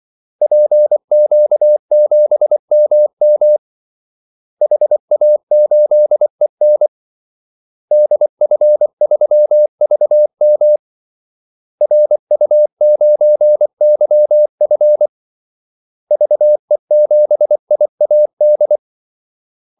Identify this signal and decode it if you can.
Morse code, heard as a keyed tone, about 24 wpm, 600 Hz. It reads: PQ7MM HA8EN DF3VM RU9YF VE7IAD
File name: Callsigns-24wpm